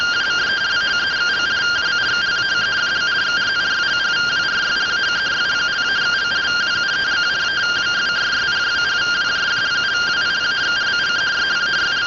Two-tone teleprinter signal, about 170 Hz shift.